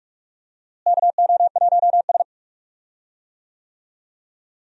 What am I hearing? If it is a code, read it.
Morse code, heard as a keyed tone, about 46 wpm, 700 Hz. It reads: KO1S
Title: USA Call Signs - 4 Characters - Repeated 3x - 45wpm